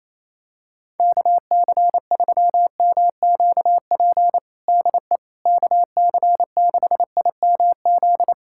Morse code, heard as a keyed tone, about 28 wpm, 700 Hz. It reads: KC3MQP DE KC6IMZ
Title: USA Callsign DE Callsign 28wpm